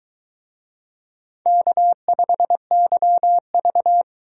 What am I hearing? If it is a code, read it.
Morse code, heard as a keyed tone, about 23 wpm, 700 Hz. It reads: K5YV